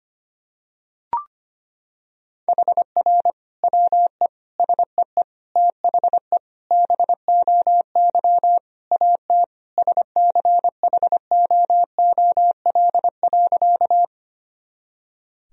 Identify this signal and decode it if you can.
Morse code, heard as a keyed tone, about 25 wpm, 700 Hz. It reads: HR WE SEE THE BOY AT SCHOOL.
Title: Sentences from Top 300 Words and Abbreviations 25wpm